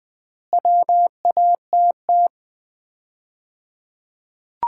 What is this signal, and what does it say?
Morse code, heard as a keyed tone, about 20 wpm, 700 Hz. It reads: WATT
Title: Common QSO Phrases - 3x Word-Spacing - 20wpm